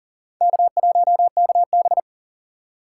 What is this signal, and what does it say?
Morse code, heard as a keyed tone, about 40 wpm, 700 Hz. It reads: K1KB
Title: USA Call Signs - 4 Characters - 4x Character Spacing - 40wpm